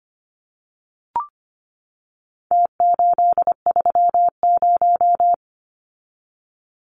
Morse code, heard as a keyed tone, about 25 wpm, 700 Hz. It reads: T830
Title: Numbers - Contest Style - 4-Digit 25wpm